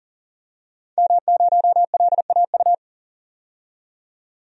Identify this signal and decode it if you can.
Morse code, heard as a keyed tone, about 40 wpm, 700 Hz. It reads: M0LAU
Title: DXCC-100 Call Signs - 5 Characters - 40wpm